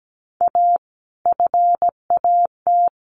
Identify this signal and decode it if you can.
Morse code, heard as a keyed tone, about 17 wpm, 700 Hz. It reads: A FAT
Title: Sentences from Top 1100 Words 17wpm